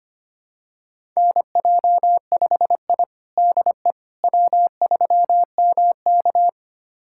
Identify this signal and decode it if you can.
Morse code, heard as a keyed tone, about 25 wpm, 700 Hz. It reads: NJ5I DE W3MK
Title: USA Callsign DE Callsign - 4 Characters - 25wpm